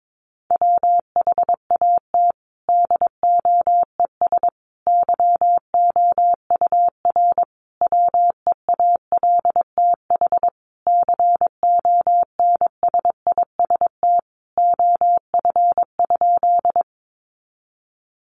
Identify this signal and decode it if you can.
Morse code, heard as a keyed tone, about 22 wpm, 700 Hz. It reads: WHAT DOES YOUR WEALTH CONSIST OF?
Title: Sentences from Top 1400 Words 22wpm